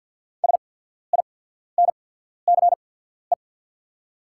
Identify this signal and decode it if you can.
Morse code, heard as a keyed tone, about 48 wpm, spaced as for 14 wpm, 700 Hz. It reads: SINCE
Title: 8x Character Spacing - Top 300 Words - 50wpm